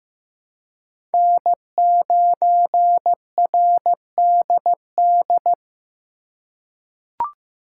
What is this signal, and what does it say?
Morse code, heard as a keyed tone, about 15 wpm, 700 Hz. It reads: N9RDD
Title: USA Call Signs - 5 Characters - 2x Character Spacing - 15wpm